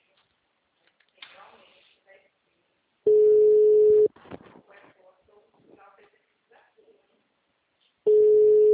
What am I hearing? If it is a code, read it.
Ringback tone (Europe, 425 Hz), with speech.